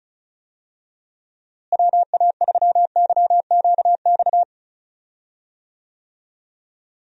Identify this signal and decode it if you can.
Morse code, heard as a keyed tone, about 36 wpm, 700 Hz. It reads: WA3YQX
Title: USA Call Signs - 6 Characters - Repeated 2x - 35wpm